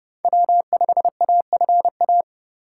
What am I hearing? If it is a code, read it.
Morse code, heard as a keyed tone, about 29 wpm, 700 Hz. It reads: W5AFA